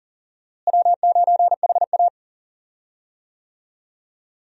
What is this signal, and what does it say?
Morse code, heard as a keyed tone, about 40 wpm, 700 Hz. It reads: W9HA
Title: USA Call Signs - 4 Characters - Repeated 3x - 40wpm